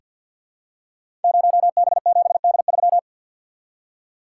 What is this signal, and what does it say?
Morse code, heard as a keyed tone, about 48 wpm, 700 Hz. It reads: 0B7D3
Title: Code Groups - 5 Letter:Numbers - 50wpm